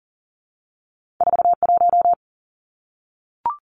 Morse code, heard as a keyed tone, about 41 wpm, 700 Hz. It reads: 4 1
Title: Two Challenging Letter-Number Combinations 40wpm